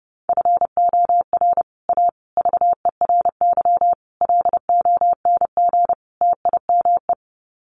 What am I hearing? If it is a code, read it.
Morse code, heard as a keyed tone, about 30 wpm, 700 Hz. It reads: FOR A VERY LONG TIME
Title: Sets of 5 Words 30wpm